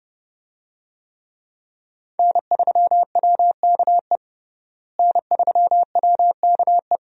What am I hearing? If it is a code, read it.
Morse code, heard as a keyed tone, about 32 wpm, 700 Hz. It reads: N3WKE N3WKE
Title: USA Call Signs - 5 Characters - Repeated 2x - 30wpm